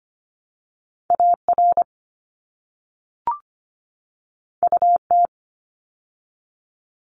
Morse code, heard as a keyed tone, about 25 wpm, 700 Hz. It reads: AR UT
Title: ARRL Sections 25wpm